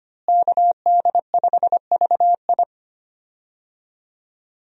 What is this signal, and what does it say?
Morse code, heard as a keyed tone, about 25 wpm, 700 Hz. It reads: KD5VI
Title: USA Call Signs - 5 Characters - Repeated 3x - 25wpm